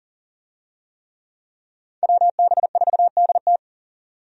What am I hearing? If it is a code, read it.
Morse code, heard as a keyed tone, about 40 wpm, 700 Hz. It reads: WB4DT
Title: USA Call Signs - 5 Characters - 40wpm